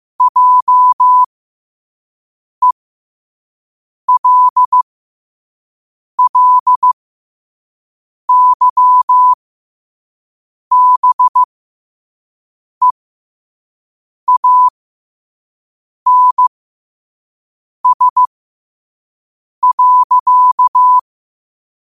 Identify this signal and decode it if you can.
Morse code, heard as a keyed tone, about 15 wpm, spaced as for 5 wpm, 1000 Hz. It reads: JELLYBEANS.
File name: 5wpm